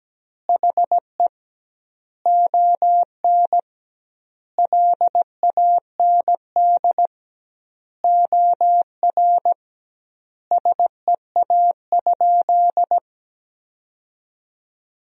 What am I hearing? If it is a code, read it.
Morse code, heard as a keyed tone, about 17 wpm, 700 Hz. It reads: HE ON LAND OR SEA?
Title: 2x Word-Spacing - Sentences from Top 500 Words - 17wpm